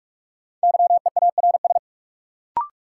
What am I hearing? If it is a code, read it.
Morse code, heard as a keyed tone, about 46 wpm, 700 Hz. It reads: YEARS
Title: Sentences from Top 200 Words - New Word Drill - 45wpm